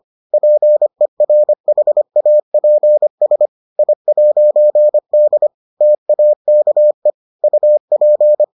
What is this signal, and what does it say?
Morse code, heard as a keyed tone, about 25 wpm, 600 Hz. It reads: PERHAPS I'D TAKE UP